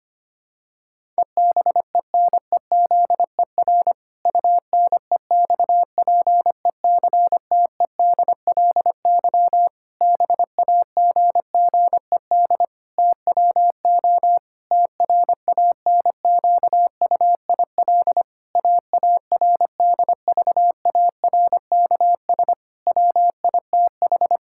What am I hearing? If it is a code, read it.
Morse code, heard as a keyed tone, about 25 wpm, 700 Hz. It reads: EBENEZER UNEXPECTEDLY BAGGED TWO TRANQUIL AARDVARKS WITH
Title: Pangram Challenge v2 - 25wpm